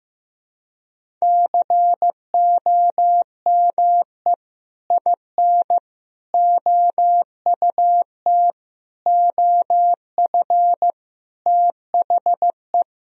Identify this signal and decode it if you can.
Morse code, heard as a keyed tone, about 15 wpm, 700 Hz. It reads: COME IN OUT OF THE
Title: Sentences from Top 1000 Words - Review Part 1 - 15wpm